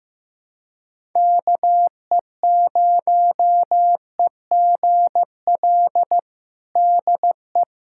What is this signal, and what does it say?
Morse code, heard as a keyed tone, about 15 wpm, 700 Hz. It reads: KE0EGL DE
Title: USA Callsign DE Callsign 15wpm